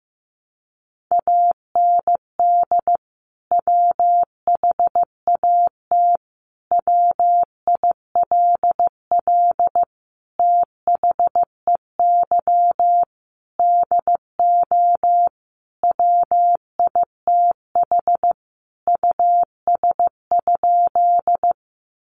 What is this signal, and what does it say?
Morse code, heard as a keyed tone, about 15 wpm, 700 Hz. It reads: AND WHAT WILL THEY DO WITH US?
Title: Sentences from Top 200 Words 15wpm